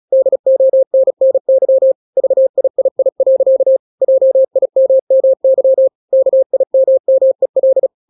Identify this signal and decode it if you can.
Morse code, heard as a keyed tone, about 36 wpm, 530 Hz. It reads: DONNY VIII. JIMMY KIMMEL
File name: daily / 35wpm